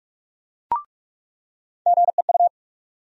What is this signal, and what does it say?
Morse code, heard as a keyed tone, about 44 wpm, 700 Hz. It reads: GEU
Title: USA Call Sign Postfixes - 3 Letters 45wpm